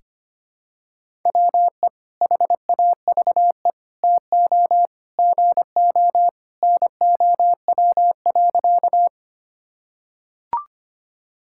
Morse code, heard as a keyed tone, about 25 wpm, 700 Hz. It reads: WE HAVE TO GO NOW.
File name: practice / 25wpm / Sentences with 3-Word Phrases - 25wpm